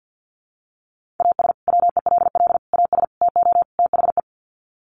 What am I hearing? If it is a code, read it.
Morse code, heard as a keyed tone, about 50 wpm, 700 Hz. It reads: AS WELL AS TO THE
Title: Sets of 5 Words Encore2 50wpm